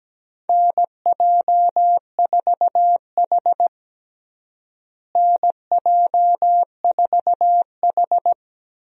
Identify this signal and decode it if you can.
Morse code, heard as a keyed tone, about 17 wpm, 700 Hz. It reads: NJ4H NJ4H
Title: USA Call Signs - 4 Characters - Repeated 2x - 17wpm